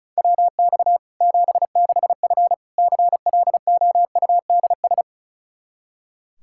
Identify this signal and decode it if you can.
Morse code, heard as a keyed tone, about 34 wpm, 700 Hz. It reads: WX 76F CLOUDS
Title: Practice QSO - Exchange 2 - Info - 35wpm